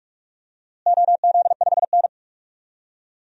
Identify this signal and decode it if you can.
Morse code, heard as a keyed tone, about 46 wpm, 700 Hz. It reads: OZ5N
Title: DXCC-100 Call Signs - 4 Characters - Repeated 3x - 45wpm